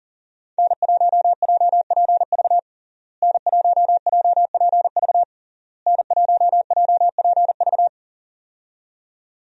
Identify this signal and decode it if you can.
Morse code, heard as a keyed tone, about 37 wpm, 700 Hz. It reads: N1JPV N1JPV N1JPV
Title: USA Call Signs - 5 Characters - Repeated 3x - 40wpm